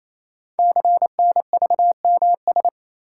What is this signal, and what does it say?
Morse code, heard as a keyed tone, about 27 wpm, 700 Hz. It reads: CNVMS
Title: Code Groups - 5 Letters - 28wpm